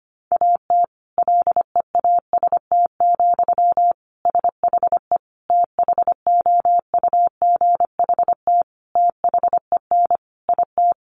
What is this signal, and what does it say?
Morse code, heard as a keyed tone, about 25 wpm, 700 Hz. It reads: AT LEAST, SHE THOUGHT THEN IT
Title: Sentences from Top 1400 Words 25wpm